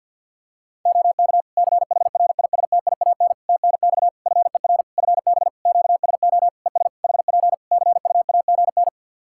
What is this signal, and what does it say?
Morse code, heard as a keyed tone, about 50 wpm, 700 Hz. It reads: OK CHRISTIAN TNX FER FB QSO ES HP CUAGN
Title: Practice QSO - Exchange 4 - Ending - 50wpm